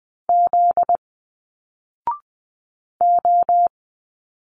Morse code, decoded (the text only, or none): Z O